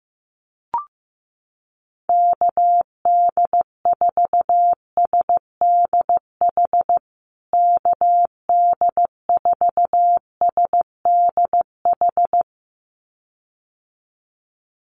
Morse code, heard as a keyed tone, about 15 wpm, 700 Hz. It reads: KD4SDH KD4SDH
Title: USA Call Signs Repeated 2x 15wpm